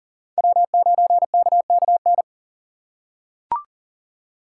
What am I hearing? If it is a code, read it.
Morse code, heard as a keyed tone, about 38 wpm, 700 Hz. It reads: W9KKN
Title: USA Call Signs - 4x Character Spacing - 40wpm